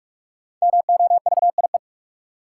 Morse code, heard as a keyed tone, about 46 wpm, 700 Hz. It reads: MOVIE
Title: Sentences from Top 1000 Words - Review Part 5 - New Word Drill - 45wpm